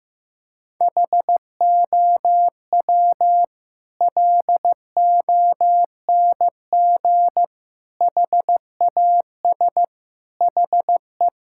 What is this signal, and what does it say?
Morse code, heard as a keyed tone, about 15 wpm, 700 Hz. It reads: HOW LONG HAS HE
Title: Sentences from Top 1000 Words - Review Part 6 - 15wpm